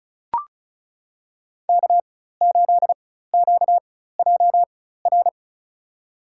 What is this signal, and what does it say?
Morse code, heard as a keyed tone, about 34 wpm, spaced as for 16 wpm, 700 Hz. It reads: K8QJR
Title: USA Call Signs - 5 Characters - 4x Character Spacing - 35wpm